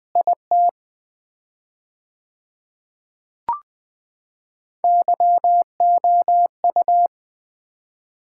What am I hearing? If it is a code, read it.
Morse code, heard as a keyed tone, about 20 wpm, 700 Hz. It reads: IT YOU
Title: 4x Word-Spacing - Sets of 2 Words - 20wpm